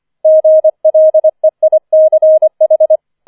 Morse code, heard as a keyed tone, about 25 wpm, 630 Hz. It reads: GLEICH
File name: 25wpm